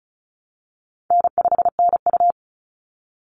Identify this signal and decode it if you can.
Morse code, heard as a keyed tone, about 34 wpm, 700 Hz. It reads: N5NU
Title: Call Signs Repeated 3x - 35wpm